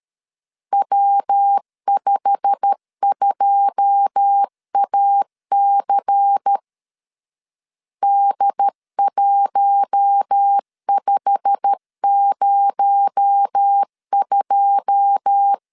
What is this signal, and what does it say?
Morse code, heard as a keyed tone, about 13 wpm, 800 Hz. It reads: W52AC D1502
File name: LEZIONI / 9 2 12wpm 2301